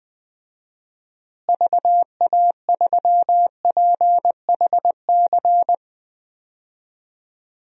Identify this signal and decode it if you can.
Morse code, heard as a keyed tone, about 19 wpm, 700 Hz. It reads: VA3PHC
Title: DXCC-100 Call Signs - 6 Characters - Repeated 3x - 20wpm